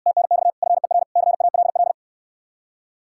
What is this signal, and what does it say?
Morse code, heard as a keyed tone, about 69 wpm, 700 Hz. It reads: TNX FER CALL